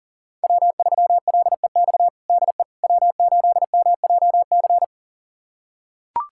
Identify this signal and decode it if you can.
Morse code, heard as a keyed tone, about 40 wpm, 700 Hz. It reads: W3LEX DE W8MJC
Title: USA Callsign DE Callsign - 5 Characters - 40wpm